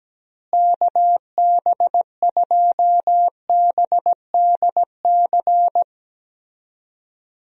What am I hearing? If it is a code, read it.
Morse code, heard as a keyed tone, about 17 wpm, 700 Hz. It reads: KB2BDC
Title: USA Call Signs - 6 Characters - Repeated 2x - 17wpm